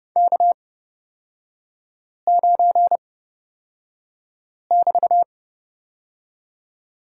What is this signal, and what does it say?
Morse code, heard as a keyed tone, about 30 wpm, spaced as for 5 wpm, 700 Hz. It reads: K9<BT>
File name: Lesson 38e - Letter q - Three Characters 30wpm